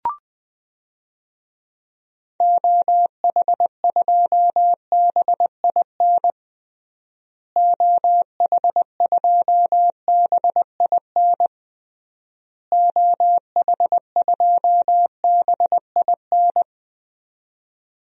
Morse code, heard as a keyed tone, about 20 wpm, 700 Hz. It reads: OH2BIN OH2BIN OH2BIN
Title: DXCC-100 Call Signs - 6 Characters - Repeated 3x - 20wpm